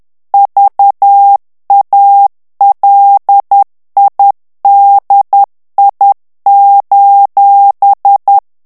Morse code, heard as a keyed tone, about 10 wpm, 800 Hz. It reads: VALIDI:
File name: ITALIEN texte clair 2 vitesse 10 mots minute